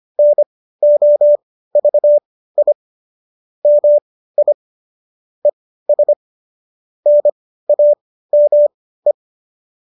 Morse code, heard as a keyed tone, about 25 wpm, spaced as for 15 wpm, 590 Hz. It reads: NOVI MI ES NAME